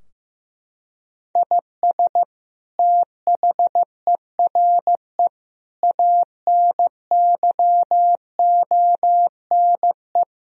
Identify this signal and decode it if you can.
Morse code, heard as a keyed tone, about 15 wpm, 700 Hz. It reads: IS THERE ANYONE
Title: Sentences from Top 1000 Words - Review Part 3 - 15wpm